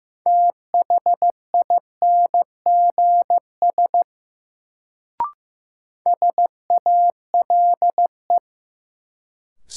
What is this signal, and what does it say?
Morse code, heard as a keyed tone, about 15 wpm, 700 Hz. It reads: THINGS SALE
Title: Sentences from Top 1400 Words - New Word Drill Alt - 15wpm